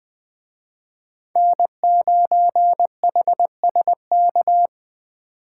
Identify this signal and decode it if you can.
Morse code, heard as a keyed tone, about 21 wpm, 700 Hz. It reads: N9HSK